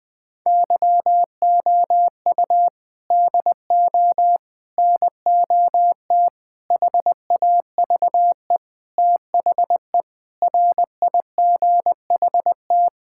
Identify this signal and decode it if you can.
Morse code, heard as a keyed tone, about 19 wpm, 700 Hz. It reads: YOU DO NOT HAVE THE RIGHT
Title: Sentences with 5-Word Phrases - 20wpm